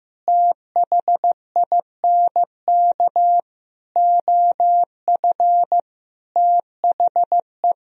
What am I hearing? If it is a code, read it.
Morse code, heard as a keyed tone, about 15 wpm, 700 Hz. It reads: THINK OF THE
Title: Sentences from Top 1000 Words - Review Part 1 - 15wpm